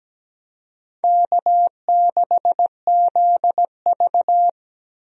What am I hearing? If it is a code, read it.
Morse code, heard as a keyed tone, about 17 wpm, 700 Hz. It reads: K6ZV
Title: USA Call Signs - 4 Characters - 17wpm